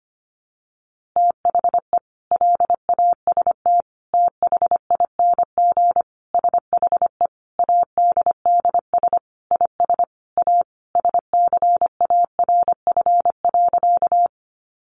Morse code, heard as a keyed tone, about 25 wpm, 700 Hz. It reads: THE LAST THING SHE ADDS IS A SCARF.
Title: Easy Phrases 25wpm